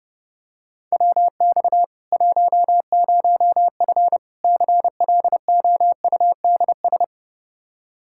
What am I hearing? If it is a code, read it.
Morse code, heard as a keyed tone, about 30 wpm, 700 Hz. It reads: WX 10F CLOUDS
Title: Practice QSO - Exchange 2 - Info - 30wpm